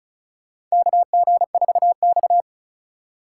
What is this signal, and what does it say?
Morse code, heard as a keyed tone, about 36 wpm, 700 Hz. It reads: KG4X
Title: USA Call Signs - 4 Characters - 4x Character Spacing - 35wpm